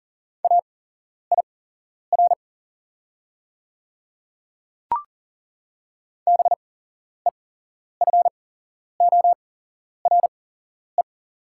Morse code, heard as a keyed tone, about 41 wpm, spaced as for 11 wpm, 700 Hz. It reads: AIR BEFORE
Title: 8x Character Spacing - Top 300 Words - 40wpm